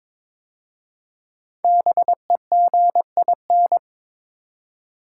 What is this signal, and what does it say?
Morse code, heard as a keyed tone, about 22 wpm, 700 Hz. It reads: BEGIN
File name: 5 Letter Words - First 3 Letters Given - 22wpm